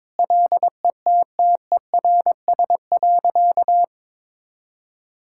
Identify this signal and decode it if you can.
Morse code, heard as a keyed tone, about 22 wpm, 690 Hz. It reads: LETTERS.
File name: practice / 22wpm / Easy Sentences with Hints - 22wpm